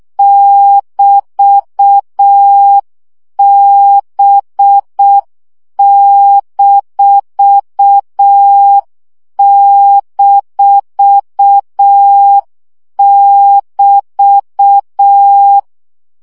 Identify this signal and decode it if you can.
Morse code, heard as a keyed tone, about 6 wpm, 800 Hz. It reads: <BT>B--<BT>
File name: CW gyakorlas b 6 egyenlo minusz 5 baud